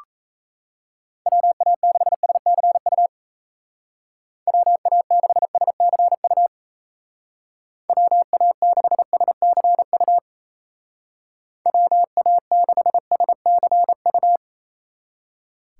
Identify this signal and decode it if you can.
Morse code, heard as a keyed tone, about 35 wpm, 700 Hz. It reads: WA6SCU WA6SCU WA6SCU WA6SCU
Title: Speed Racing - USA 2x3 Call Signs 28wpm